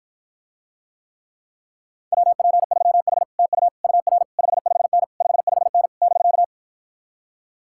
Morse code, heard as a keyed tone, about 52 wpm, 700 Hz. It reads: WP3H TU UR 55N 55N <BK>